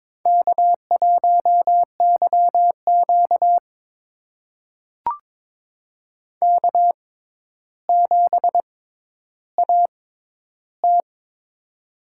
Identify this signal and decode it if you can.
Morse code, heard as a keyed tone, about 22 wpm, spaced as for 11 wpm, 700 Hz. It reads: K1YQ K7AT
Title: USA Call Signs - 4 Characters - 6x Character Spacing - 22wpm